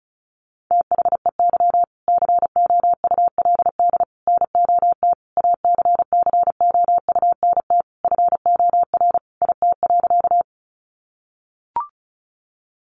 Morse code, heard as a keyed tone, about 34 wpm, 700 Hz. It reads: THEY COULD NOT ACCOUNT FOR IT.
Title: Sentences from Top 900 Words 35wpm